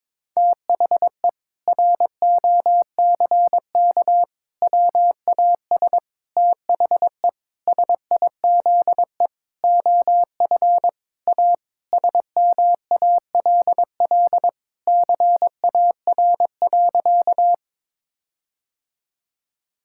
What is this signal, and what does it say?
Morse code, heard as a keyed tone, about 22 wpm, 700 Hz. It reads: THE ROCK WAS THE SIZE OF A SMALL CAR.
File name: practice / 22wpm / Sentences with 4-Word Phrases - 22wpm